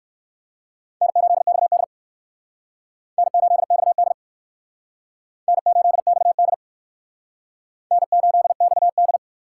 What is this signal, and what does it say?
Morse code, heard as a keyed tone, about 62 wpm, 700 Hz. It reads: N8XD N8XD N8XD N8XD